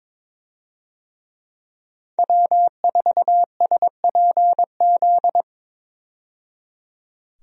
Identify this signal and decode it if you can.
Morse code, heard as a keyed tone, about 21 wpm, 700 Hz. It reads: W4SPZ
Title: USA Call Signs - 5 Characters - 22wpm